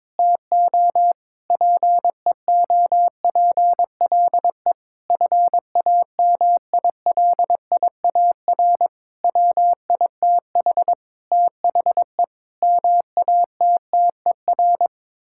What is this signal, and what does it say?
Morse code, heard as a keyed tone, about 22 wpm, 700 Hz. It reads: TO PEOPLE FAMILIAR WITH THE MATTER
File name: practice / 22wpm / Sets of 7 Words - 22wpm